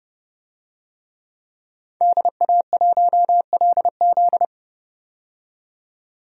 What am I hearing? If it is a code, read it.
Morse code, heard as a keyed tone, about 29 wpm, 700 Hz. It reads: DA1LZ
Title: USA Call Signs - 5 Characters - 30wpm